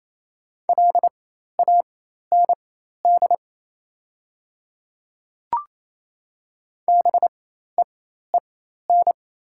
Morse code, heard as a keyed tone, about 28 wpm, spaced as for 13 wpm, 690 Hz. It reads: LAND BEEN
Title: 4x Character Spacing - Top 200 Words - 28wpm